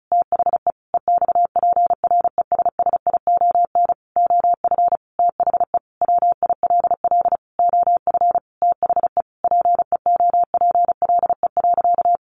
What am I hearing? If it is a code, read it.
Morse code, heard as a keyed tone, about 34 wpm, 700 Hz. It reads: THE EXPRESSION OF THE WILL OF THE PEOPLE.
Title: Sentences from Top 1500 Words 35wpm